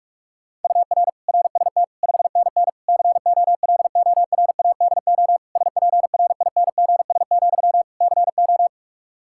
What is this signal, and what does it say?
Morse code, heard as a keyed tone, about 46 wpm, 700 Hz. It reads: UR RST 5NN COLORADO SPRINGS, CO
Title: Practice QSO - Exchange 1 - Beginning - 45wpm